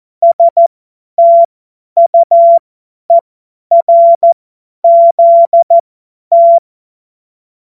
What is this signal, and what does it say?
Morse code, heard as a keyed tone, about 14 wpm, spaced as for 10 wpm, 680 Hz. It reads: STUERZT
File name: Morsekurs / Witze / Witz 31. Mai 2022 50 F70